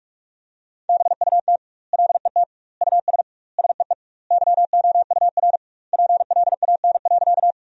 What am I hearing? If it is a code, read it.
Morse code, heard as a keyed tone, about 46 wpm, 700 Hz. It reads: BUT LET US SEE YOUR PLAN.
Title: 2x Word-Spacing - Sentences from Top 500 Words - 45wpm